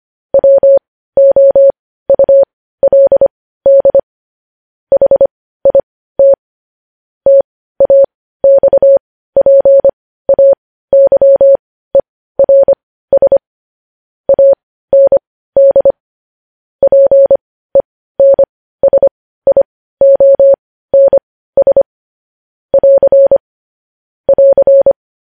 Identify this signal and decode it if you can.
Morse code, heard as a keyed tone, about 25 wpm, spaced as for 15 wpm, 550 Hz. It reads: WOULD HIT TAXPAYERS AND PENSIONS <AR> <AR>
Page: news headlines in morse code at 15 wpm